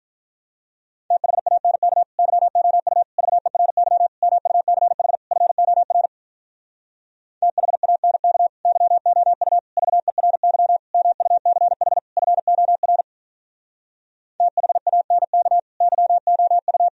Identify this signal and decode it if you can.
Morse code, heard as a keyed tone, about 46 wpm, 700 Hz. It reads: THANK YOU VERY MUCH FOR THANK YOU VERY MUCH FOR THANK YOU